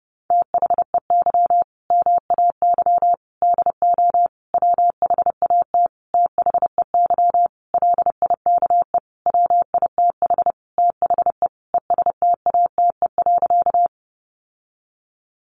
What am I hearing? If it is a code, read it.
Morse code, heard as a keyed tone, about 28 wpm, 700 Hz. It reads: THEY MAY DO WHAT THEY LIKE WITH THE ESTATE.
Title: Sentences from Top 1200 Words 30wpm